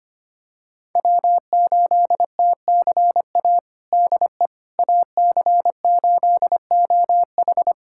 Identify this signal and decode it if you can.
Morse code, heard as a keyed tone, about 25 wpm, 700 Hz. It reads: W8TCA DE AC8OH
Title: USA Callsign DE Callsign - 5 Characters - 25wpm